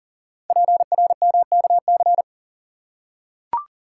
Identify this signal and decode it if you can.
Morse code, heard as a keyed tone, about 41 wpm, 690 Hz. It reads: PRMKC
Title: Code Groups - 5 Letters - 40wpm